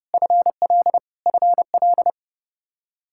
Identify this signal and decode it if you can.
Morse code, heard as a keyed tone, about 32 wpm, 700 Hz. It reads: FL FL
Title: Practice QSO - Exchange 1 - Beginning - 30wpm